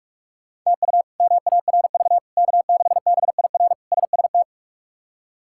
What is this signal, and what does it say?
Morse code, heard as a keyed tone, about 42 wpm, 700 Hz. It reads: TU MARV K6BIR SST
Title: K1USN SST QSO Practice - 45wpm